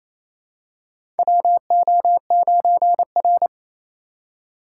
Morse code, heard as a keyed tone, about 27 wpm, 700 Hz. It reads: WO9R
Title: USA Call Signs - 4 Characters - 2x Character Spacing - 28wpm